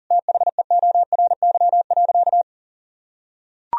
Morse code, heard as a keyed tone, about 40 wpm, 700 Hz. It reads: THEORY.
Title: Sentences from Top 1000 Words - Review Part 4 - 40wpm